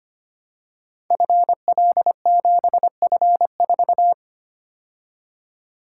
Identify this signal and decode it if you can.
Morse code, heard as a keyed tone, about 25 wpm, 700 Hz. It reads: FL7F4
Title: Code Groups - 5 Letter:Numbers - 25wpm